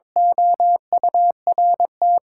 Morse code, heard as a keyed tone, about 22 wpm, 700 Hz. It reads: OURT
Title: Sentences from Top 1600 Words 22wpm